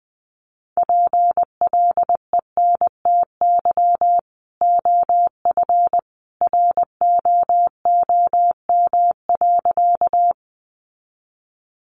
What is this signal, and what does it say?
Morse code, heard as a keyed tone, about 20 wpm, 700 Hz. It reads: PLENTY OF ROOM.
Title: Sentences from Top 1300 Words 20wpm